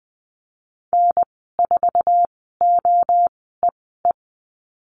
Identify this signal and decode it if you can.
Morse code, heard as a keyed tone, about 20 wpm, spaced as for 15 wpm, 700 Hz. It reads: N4OEE
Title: 062 Lesson 11g - Letter h - Call Signs 20wpm